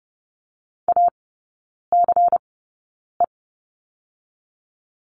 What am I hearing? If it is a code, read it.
Morse code, heard as a keyed tone, about 30 wpm, spaced as for 9 wpm, 690 Hz. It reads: ACE
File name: Lesson 17e - Letter u - Three Characters 30wpm